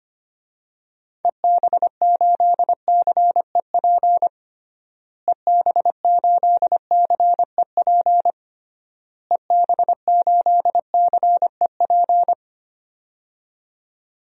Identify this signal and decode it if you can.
Morse code, heard as a keyed tone, about 25 wpm, 700 Hz. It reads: EB8CEP EB8CEP EB8CEP
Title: DXCC-100 Call Signs - 6 Characters - Repeated 3x - 25wpm